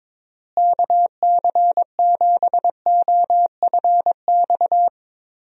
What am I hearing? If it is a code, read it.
Morse code, heard as a keyed tone, about 22 wpm, 700 Hz. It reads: KC7OFX
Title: USA Call Signs - 6 Characters - Repeated 3x - 22wpm